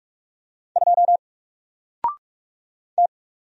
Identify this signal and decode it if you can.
Morse code, heard as a keyed tone, about 46 wpm, 700 Hz. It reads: 2 T
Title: Single Character (Letter:Number:Punctuation:Prosign) Alt 45wpm